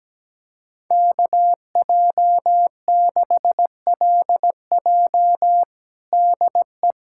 Morse code, heard as a keyed tone, about 17 wpm, 700 Hz. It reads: KJ6LJ DE
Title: USA Callsign DE Callsign - 5 Characters - 17wpm